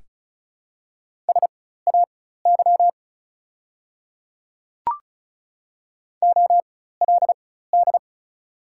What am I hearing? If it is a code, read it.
Morse code, heard as a keyed tone, about 36 wpm, spaced as for 16 wpm, 700 Hz. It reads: SAY OLD
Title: 4x Character Spacing - Top 200 Words - 35wpm